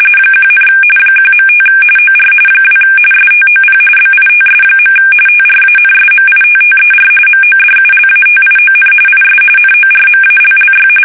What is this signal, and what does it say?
RTTY, about 45.45 baud. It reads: 1, 1.5 OR 2 STOP BITS. TRANSMITTER MODULATION IS NORMALLY
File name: fsk850 / teletype-RTTY